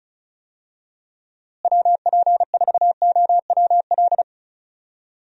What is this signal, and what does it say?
Morse code, heard as a keyed tone, about 36 wpm, 700 Hz. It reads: WP4OWL
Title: USA Call Signs - 6 Characters - Repeated 2x - 35wpm